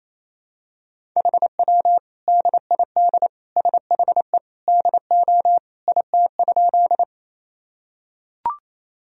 Morse code, heard as a keyed tone, about 28 wpm, 700 Hz. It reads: HW DID SHE DO IT?
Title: Sentences from Top 100 Words and Abbreviations 28wpm